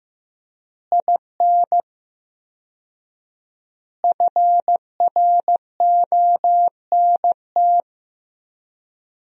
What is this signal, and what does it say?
Morse code, heard as a keyed tone, about 15 wpm, 700 Hz. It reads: IN FRONT
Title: 4x Word-Spacing - Sentences from Top 500 Words - 15wpm